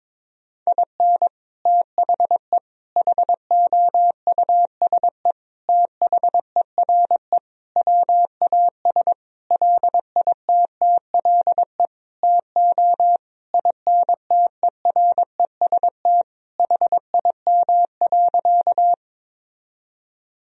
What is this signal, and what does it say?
Morse code, heard as a keyed tone, about 22 wpm, 700 Hz. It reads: IN THE HOUSE THERE WAS LITTLE TO INTEREST HIM.
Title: Sentences from Top 1000 Words - Review Part 2 - 22wpm